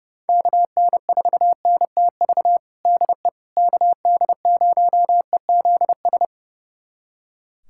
Morse code, heard as a keyed tone, about 30 wpm, 700 Hz. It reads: KN4NTV DE KD0EZS
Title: USA Callsign DE Callsign 30wpm